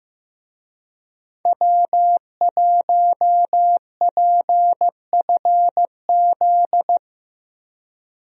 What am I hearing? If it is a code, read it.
Morse code, heard as a keyed tone, about 15 wpm, 700 Hz. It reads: W1PFZ
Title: USA Call Signs - 5 Characters - Repeated 3x - 15wpm